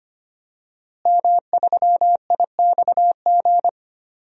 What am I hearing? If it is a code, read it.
Morse code, heard as a keyed tone, about 25 wpm, 700 Hz. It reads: M3IXG
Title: DXCC-100 Call Signs - 5 Characters - Repeated 3x - 25wpm